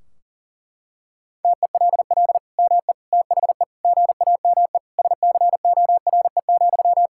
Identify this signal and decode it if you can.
Morse code, heard as a keyed tone, about 40 wpm, 700 Hz. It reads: TELL ME THE GAME SCORE,
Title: Sentences from Top 1600 Words 40wpm